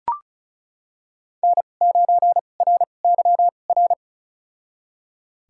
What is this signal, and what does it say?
Morse code, heard as a keyed tone, about 36 wpm, spaced as for 25 wpm, 700 Hz. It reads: N9RYR
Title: USA Call Signs - 5 Characters - 2x Character Spacing - 35wpm